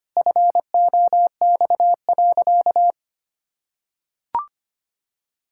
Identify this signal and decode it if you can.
Morse code, heard as a keyed tone, about 25 wpm, 700 Hz. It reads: FOX.
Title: Pangram Challenge v2 - 25wpm